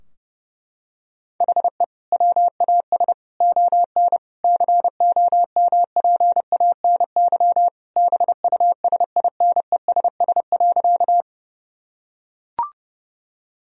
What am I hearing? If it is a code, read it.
Morse code, heard as a keyed tone, about 30 wpm, 700 Hz. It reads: HE WAS ON COMPANY BUSINESS.
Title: Sentences from Top 1000 Words - Review Part 3 - 30wpm